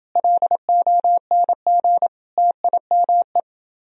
Morse code, heard as a keyed tone, about 31 wpm, 700 Hz. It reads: LONG TIME